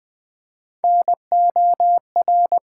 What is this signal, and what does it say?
Morse code, heard as a keyed tone, about 20 wpm, 700 Hz. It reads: NOR
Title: Build A Word - 1 Letter At A Time - 3-Letter Words - 20wpm